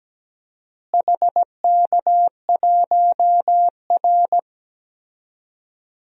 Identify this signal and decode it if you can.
Morse code, heard as a keyed tone, about 17 wpm, 700 Hz. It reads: HK1R
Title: DXCC-100 Call Signs - 4 Characters - Repeated 3x - 17wpm